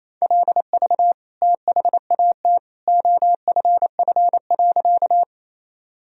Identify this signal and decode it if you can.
Morse code, heard as a keyed tone, about 27 wpm, 700 Hz. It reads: LV THAT OFF.
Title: Sentences from Top 400 Words and Abbreviations 28wpm